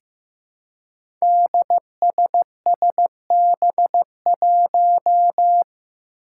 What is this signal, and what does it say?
Morse code, heard as a keyed tone, about 15 wpm, 700 Hz. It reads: DSSB1
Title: Five Challenging Letter-Number Combinations 15wpm